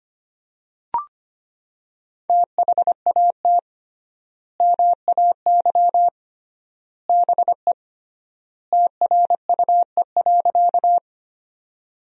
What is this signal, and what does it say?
Morse code, heard as a keyed tone, about 25 wpm, 700 Hz. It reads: THAT MAY BE TRUE.
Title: 3x Word-Spacing - Sentences from Top 400 Words - 25wpm